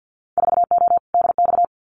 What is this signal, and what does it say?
Morse code, heard as a keyed tone, about 50 wpm, 700 Hz. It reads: 4O DX
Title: Winter Field Day Exchanges Encore 50wpm